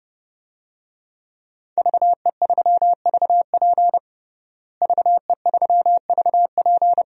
Morse code, heard as a keyed tone, about 30 wpm, 700 Hz. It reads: VE3VP VE3VP
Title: DXCC-100 Call Signs - 5 Characters - Repeated 2x - 30wpm